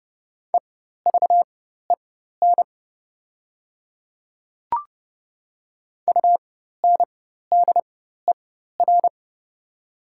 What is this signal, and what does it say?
Morse code, heard as a keyed tone, about 28 wpm, spaced as for 14 wpm, 700 Hz. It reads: EVEN UNDER
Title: 4x Character Spacing - Top 200 Words - 30wpm